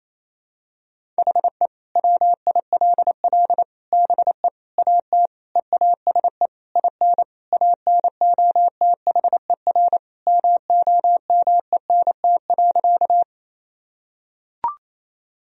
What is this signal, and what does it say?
Morse code, heard as a keyed tone, about 28 wpm, 700 Hz. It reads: HE WILL BE AT EASE IN ANOTHER MOMENT.
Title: Sentences from Top 1000 Words - Review Part 1 - 28wpm